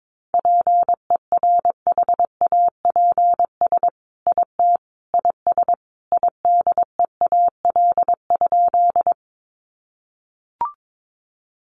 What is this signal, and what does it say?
Morse code, heard as a keyed tone, about 22 wpm, 700 Hz. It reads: PERHAPS IT IS IDEAL?
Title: Sentences from Top 1400 Words 22wpm